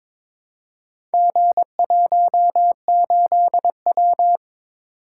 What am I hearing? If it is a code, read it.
Morse code, heard as a keyed tone, about 22 wpm, 700 Hz. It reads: G18W
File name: practice / 22wpm / Four Letter-Number Combinations - 22wpm